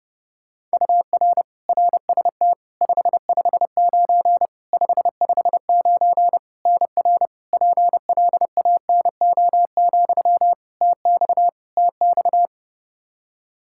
Morse code, heard as a keyed tone, about 32 wpm, 700 Hz. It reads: UR RST 559 559 NR PLANO, TX TX
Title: Practice QSO - Exchange 1 - Beginning - 30wpm